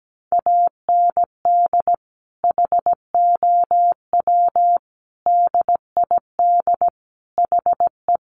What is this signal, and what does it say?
Morse code, heard as a keyed tone, about 17 wpm, 700 Hz. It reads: AND HOW DID HE
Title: Sentences from Top 400 Words 17wpm